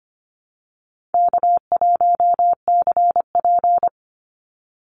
Morse code, heard as a keyed tone, about 25 wpm, 700 Hz. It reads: K1CP
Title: Easy Call Signs - 25wpm